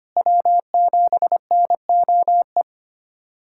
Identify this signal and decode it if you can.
Morse code, heard as a keyed tone, about 25 wpm, 700 Hz. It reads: W7NOE
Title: USA Callsign DE Callsign - 5 Characters - 25wpm